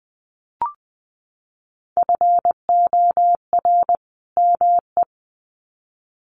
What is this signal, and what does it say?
Morse code, heard as a keyed tone, about 20 wpm, 700 Hz. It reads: FOR ME
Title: Sets of 2 Words 20wpm